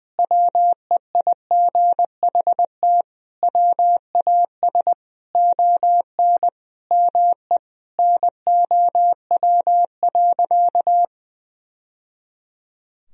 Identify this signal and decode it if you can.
Morse code, heard as a keyed tone, about 20 wpm, 700 Hz. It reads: WEIGHT WAS ON ME NOW.
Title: Sentences from Top 1000 Words - Review Part 3 - 20wpm